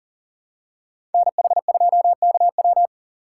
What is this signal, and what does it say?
Morse code, heard as a keyed tone, about 40 wpm, 700 Hz. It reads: NH2KW
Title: USA Call Signs - 5 Characters - 3x Character Spacing - 40wpm